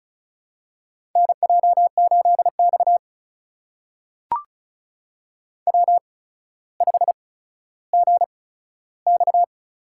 Morse code, heard as a keyed tone, about 36 wpm, spaced as for 15 wpm, 700 Hz. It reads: NJ8X W5GX
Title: USA Call Signs - 4 Characters - 8x Character Spacing - 35wpm